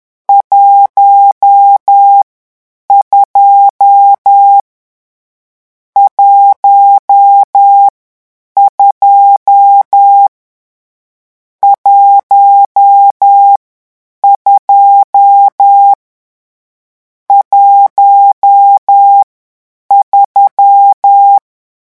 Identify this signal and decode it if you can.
Morse code, heard as a keyed tone, about 10 wpm, 800 Hz. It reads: 1 2 1 2 1 2 1 3
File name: morse mp3 / lecon10-vitesse 10